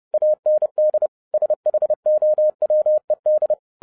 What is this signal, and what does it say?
Morse code, heard as a keyed tone, about 30 wpm, 610 Hz. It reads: AND SHOWED